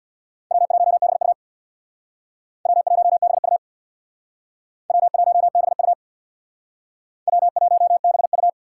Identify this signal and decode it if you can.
Morse code, heard as a keyed tone, about 64 wpm, 700 Hz. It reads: W1BU W1BU W1BU W1BU